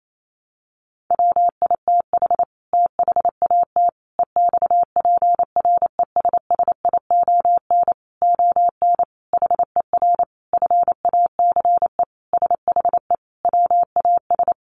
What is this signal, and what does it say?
Morse code, heard as a keyed tone, about 28 wpm, 700 Hz. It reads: WITH THAT EXPRESSION ON HER FACE SHE WAS
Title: Sentences from Top 1200 Words 28wpm